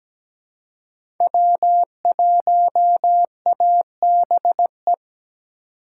Morse code, heard as a keyed tone, about 17 wpm, 700 Hz. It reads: W1ABE
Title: USA Call Signs - 2x Character Spacing - 17wpm